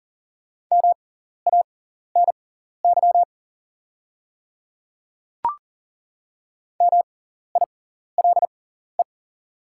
Morse code, heard as a keyed tone, about 40 wpm, spaced as for 14 wpm, 700 Hz. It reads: MANY MILE
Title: 6x Character Spacing - Top 300 Words - 40wpm